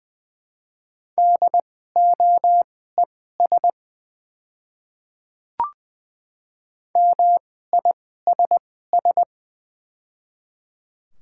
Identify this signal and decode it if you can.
Morse code, heard as a keyed tone, about 20 wpm, spaced as for 14 wpm, 700 Hz. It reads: DOES MISS
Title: 2x Character Spacing - Top 500 Words - 20wpm